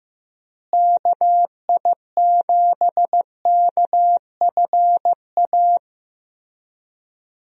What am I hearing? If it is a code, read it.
Morse code, heard as a keyed tone, about 15 wpm, 700 Hz. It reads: KI7KFA
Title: USA Call Signs - 6 Characters - Repeated 3x - 15wpm